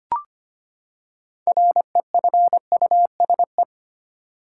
Morse code, heard as a keyed tone, about 25 wpm, 700 Hz. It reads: REFUSE
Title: Six Letter Words - 25wpm